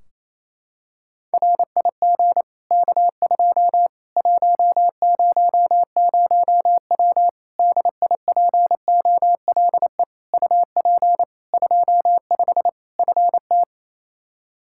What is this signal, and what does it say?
Morse code, heard as a keyed tone, about 28 wpm, 700 Hz. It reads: RIG K2 100W DIPOLE UP 25 FT
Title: Practice QSO - Exchange 2 - Info - 28wpm